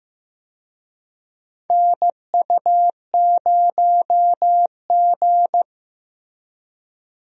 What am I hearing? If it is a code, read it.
Morse code, heard as a keyed tone, about 15 wpm, 700 Hz. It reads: NU0G
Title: USA Call Signs - 4 Characters - Repeated 2x - 15wpm